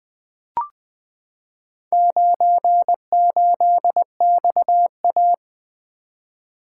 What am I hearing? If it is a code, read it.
Morse code, heard as a keyed tone, about 20 wpm, 700 Hz. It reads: 98XA
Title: Four Letter-Number Combinations - 20wpm